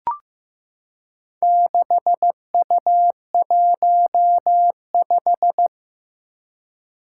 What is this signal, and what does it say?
Morse code, heard as a keyed tone, about 15 wpm, 700 Hz. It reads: 6U15
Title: Four Challenging Letter-Number Combinations 15wpm